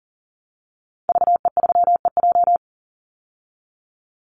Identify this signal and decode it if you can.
Morse code, heard as a keyed tone, about 40 wpm, 700 Hz. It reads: VE3EJ
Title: Call Signs Repeated 2x - 40wpm